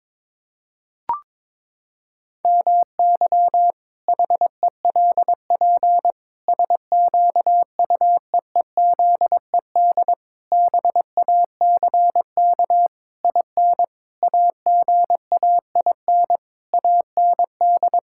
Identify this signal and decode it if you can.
Morse code, heard as a keyed tone, about 22 wpm, 700 Hz. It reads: MY HELP SQUEEZED BACK IN AGAIN AND
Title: Pangram Challenge v2 - 22wpm